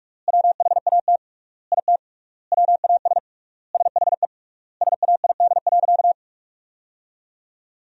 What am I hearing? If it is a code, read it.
Morse code, heard as a keyed tone, about 41 wpm, 700 Hz. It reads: WHAT IT WAS SHE SAID.
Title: 3x Word-Spacing - Sentences from Top 400 Words - 45wpm